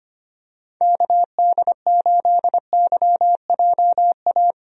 Morse code, heard as a keyed tone, about 25 wpm, 700 Hz. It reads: KD8YJA
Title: USA Call Signs - 6 Characters - 25wpm